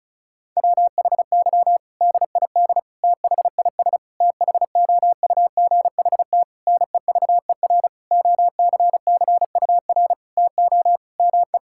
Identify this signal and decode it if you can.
Morse code, heard as a keyed tone, about 36 wpm, 700 Hz. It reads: WHY DID THIS THOUGHT NEVER OCCUR TO ME
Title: Sentences from Top 1000 Words - Review Part 6 - 35wpm